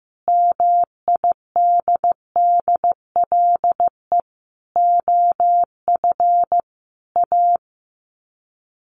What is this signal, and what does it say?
Morse code, heard as a keyed tone, about 15 wpm, 700 Hz. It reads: MIDDLE OF A
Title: Sets of 5 Words 15wpm